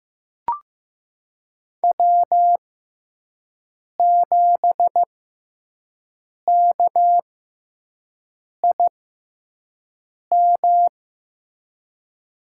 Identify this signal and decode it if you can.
Morse code, heard as a keyed tone, about 15 wpm, spaced as for 5 wpm, 700 Hz. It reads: W7KIM
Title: USA Call Signs - 5 Characters - 6x Character Spacing - 15wpm